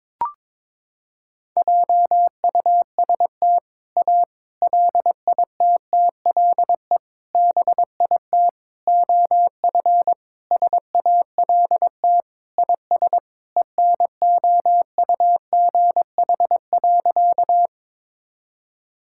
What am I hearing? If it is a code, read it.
Morse code, heard as a keyed tone, about 22 wpm, 700 Hz. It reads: JUST A LITTLE BIT OF SALT IS ENOUGH.
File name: practice / 22wpm / Sentences with 4-Word Phrases - 22wpm